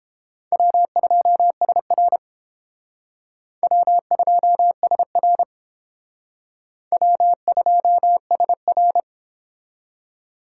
Morse code, heard as a keyed tone, about 31 wpm, 700 Hz. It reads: W2SR W2SR W2SR